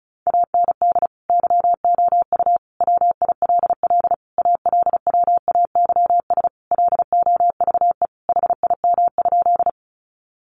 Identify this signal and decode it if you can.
Morse code, heard as a keyed tone, about 34 wpm, 700 Hz. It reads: AND YOU WILL ALWAYS LOVE HIM?
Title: Sentences from Top 400 Words 35wpm